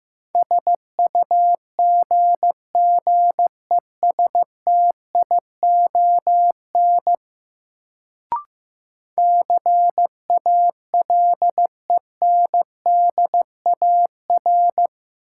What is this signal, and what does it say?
Morse code, heard as a keyed tone, about 15 wpm, 700 Hz. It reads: SUGGESTION CALENDAR
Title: Sentences from Top 1600 Words - New Word Drill Alt - 15wpm